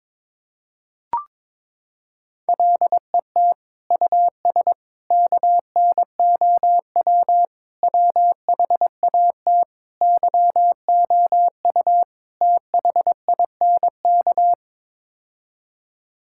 Morse code, heard as a keyed tone, about 22 wpm, 700 Hz. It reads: LET US KNOW WHAT YOU THINK
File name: practice / 22wpm / Sets of 6 Words - 22wpm